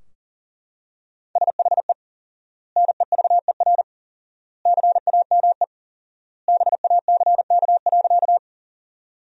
Morse code, heard as a keyed tone, about 40 wpm, 700 Hz. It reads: SHE NEVER CAME BACK.
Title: 4x Word-Spacing - Sentences from Top 300 Words - 40wpm